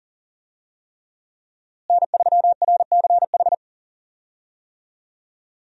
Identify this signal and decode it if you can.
Morse code, heard as a keyed tone, about 40 wpm, 700 Hz. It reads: N3RCH
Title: USA Call Signs - 5 Characters - Repeated 3x - 40wpm